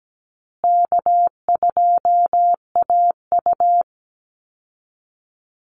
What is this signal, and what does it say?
Morse code, heard as a keyed tone, about 17 wpm, 700 Hz. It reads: K2AU
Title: Easy Call Signs Repeated 3x - 17wpm